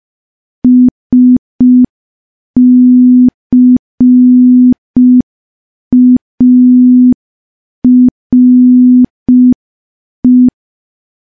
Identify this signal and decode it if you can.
Morse code, heard as a keyed tone, about 5 wpm, 250 Hz. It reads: SCARE
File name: morse / morse-training-05WPM